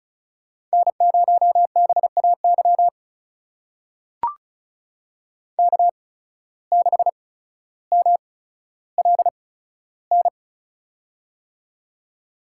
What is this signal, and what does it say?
Morse code, heard as a keyed tone, about 36 wpm, spaced as for 15 wpm, 700 Hz. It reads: N0BAY K6MLN
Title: USA Call Signs - 5 Characters - 8x Character Spacing - 35wpm